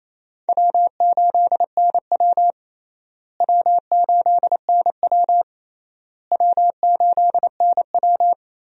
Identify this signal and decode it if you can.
Morse code, heard as a keyed tone, about 28 wpm, 700 Hz. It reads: W8NW W8NW W8NW
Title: USA Call Signs - 4 Characters - Repeated 3x - 28wpm